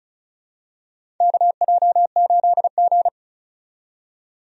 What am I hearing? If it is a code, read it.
Morse code, heard as a keyed tone, about 36 wpm, 700 Hz. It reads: KJ8G
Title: USA Call Signs - 4 Characters - 35wpm